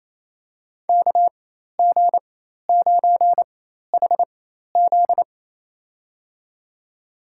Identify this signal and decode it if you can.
Morse code, heard as a keyed tone, about 28 wpm, spaced as for 13 wpm, 700 Hz. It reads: KG9HZ
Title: USA Call Signs - 5 Characters - 4x Character Spacing - 28wpm